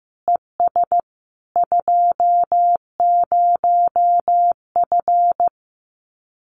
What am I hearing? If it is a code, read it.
Morse code, heard as a keyed tone, about 15 wpm, 700 Hz. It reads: ES 20F
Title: Common QSO Phrases Encore 15wpm